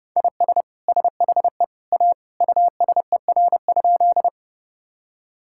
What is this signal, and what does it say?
Morse code, heard as a keyed tone, about 32 wpm, 700 Hz. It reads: IS SHE A USER?
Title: Sentences from Top 1000 Words - Review Part 5 - 30wpm